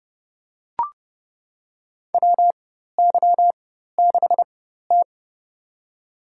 Morse code, heard as a keyed tone, about 30 wpm, spaced as for 14 wpm, 700 Hz. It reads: WY6T
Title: USA Call Signs - 4 Characters - 4x Character Spacing - 30wpm